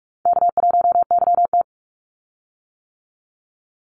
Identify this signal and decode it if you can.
Morse code, heard as a keyed tone, about 44 wpm, 700 Hz. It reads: K1YT
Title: Easy Call Signs - 45wpm